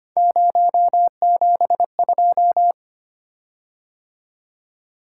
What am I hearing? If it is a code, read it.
Morse code, heard as a keyed tone, about 25 wpm, 700 Hz. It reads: 072
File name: Three Letter-Number Combinations - 25wpm